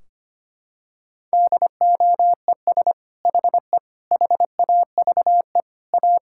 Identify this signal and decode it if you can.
Morse code, heard as a keyed tone, about 25 wpm, 700 Hz. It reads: DOES HE HAVE A
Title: Sentences from Top 1600 Words 25wpm